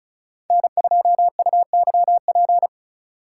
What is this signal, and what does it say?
Morse code, heard as a keyed tone, about 36 wpm, 700 Hz. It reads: N2UYP
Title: USA Call Signs - 3x Character Spacing - 35wpm